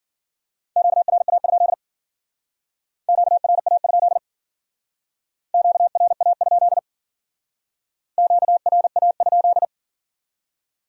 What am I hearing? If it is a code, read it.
Morse code, heard as a keyed tone, about 52 wpm, 700 Hz. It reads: QRA? QRA? QRA? QRA?